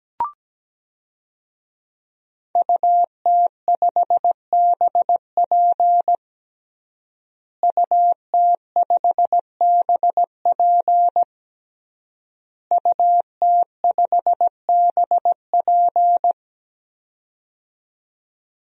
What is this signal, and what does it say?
Morse code, heard as a keyed tone, about 17 wpm, 700 Hz. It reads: UT5BP UT5BP UT5BP
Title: DXCC-100 Call Signs - 5 Characters - Repeated 3x - 17wpm